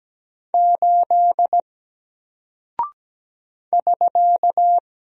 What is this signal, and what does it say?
Morse code, heard as a keyed tone, about 17 wpm, 700 Hz. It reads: 8 <SK>
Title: Single Character (Letter:Number:Punctuation:Prosign) Alt 17wpm